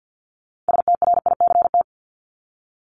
Morse code, heard as a keyed tone, about 50 wpm, 700 Hz. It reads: STRICT
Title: Top 1501-1750 Words 50wpm